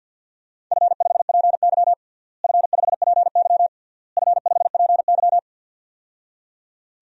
Morse code, heard as a keyed tone, about 48 wpm, 700 Hz. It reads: F5PY F5PY F5PY
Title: DXCC-100 Call Signs - 4 Characters - Repeated 3x - 50wpm